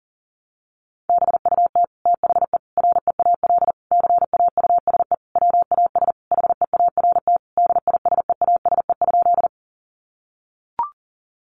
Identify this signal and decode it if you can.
Morse code, heard as a keyed tone, about 37 wpm, 700 Hz. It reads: BUT THE REAL CAUSE WAS HEART DISEASE?
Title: Sentences from Top 1200 Words 40wpm